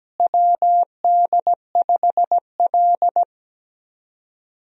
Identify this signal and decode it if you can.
Morse code, heard as a keyed tone, about 17 wpm, 700 Hz. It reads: WD5L
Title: USA Call Signs - 4 Characters - Repeated 2x - 17wpm